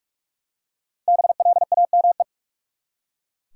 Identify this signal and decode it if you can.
Morse code, heard as a keyed tone, about 43 wpm, 700 Hz. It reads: BLAME
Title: Five Letter Words - 45wpm